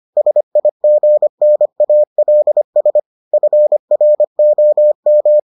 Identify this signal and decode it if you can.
Morse code, heard as a keyed tone, about 25 wpm, 600 Hz. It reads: SIGNALS FROM